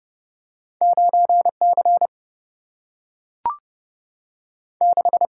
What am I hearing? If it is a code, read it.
Morse code, heard as a keyed tone, about 29 wpm, 700 Hz. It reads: 9 C 6
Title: Single-Letter Single-Number 30wpm